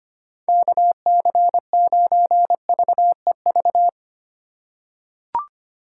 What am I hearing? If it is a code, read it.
Morse code, heard as a keyed tone, about 25 wpm, 700 Hz. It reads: KC9VEV
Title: USA Call Signs - 6 Characters - 3x Character Spacing - 25wpm